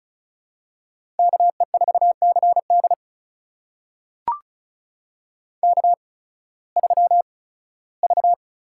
Morse code, heard as a keyed tone, about 36 wpm, spaced as for 19 wpm, 700 Hz. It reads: KE4CD K3V
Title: USA Call Signs - 8x Character Spacing - 35wpm